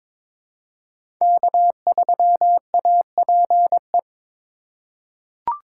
Morse code, heard as a keyed tone, about 21 wpm, 700 Hz. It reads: K3APE
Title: USA Call Signs - 5 Characters - 4x Character Spacing - 22wpm